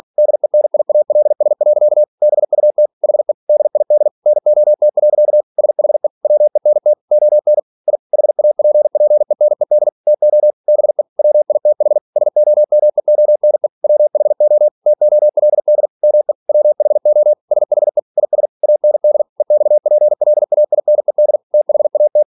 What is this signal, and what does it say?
Morse code, heard as a keyed tone, about 46 wpm, 600 Hz. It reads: DENIALS. BUT HE DID NOT. SHE WENT ON I HAPPENED TO BE WITH SOMEONE WHO TOLD ME WHO SHE IS AND EXPLAINED THAT
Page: Never 47wpm